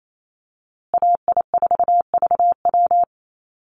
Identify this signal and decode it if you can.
Morse code, heard as a keyed tone, about 28 wpm, 700 Hz. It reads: AI4VW
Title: USA Call Signs Repeated 3x 28wpm